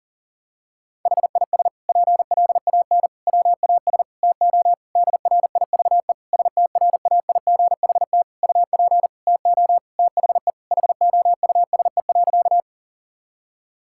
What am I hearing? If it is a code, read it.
Morse code, heard as a keyed tone, about 40 wpm, 700 Hz. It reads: HIS PLAN WAS TO DRIVE STRAIGHT UP TO THE HOUSE.
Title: Sentences from Top 1000 Words - Review Part 6 - 40wpm